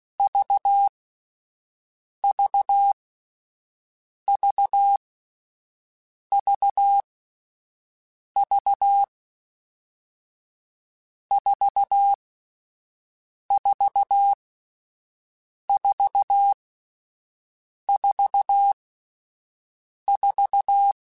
Morse code, heard as a keyed tone, about 16 wpm, spaced as for 5 wpm, 800 Hz. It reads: VVVVV 44444
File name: Difficult Combos / Difficult Combos-mp3s / AUV AUV4 @5